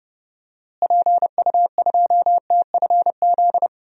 Morse code, heard as a keyed tone, about 29 wpm, 700 Hz. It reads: PU2TFZ
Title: DXCC-100 Call Signs - 6 Characters - Repeated 2x - 30wpm